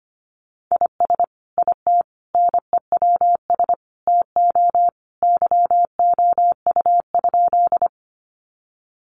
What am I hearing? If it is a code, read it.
Morse code, heard as a keyed tone, about 25 wpm, 700 Hz. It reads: IS IT NEWS TO YOU?
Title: Sentences from Top 800 Words 25wpm